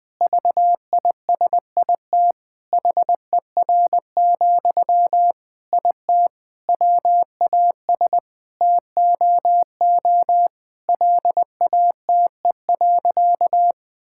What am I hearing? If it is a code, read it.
Morse code, heard as a keyed tone, about 21 wpm, 700 Hz. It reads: VISIT HER, IT WAS TOO LATE.
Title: Sentences from Top 1000 Words - Review Part 6 - 20wpm